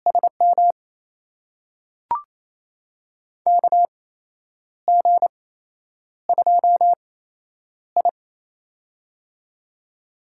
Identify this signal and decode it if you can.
Morse code, heard as a keyed tone, about 27 wpm, spaced as for 9 wpm, 700 Hz. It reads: SM KG2I
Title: USA Call Signs - 8x Character Spacing - 28wpm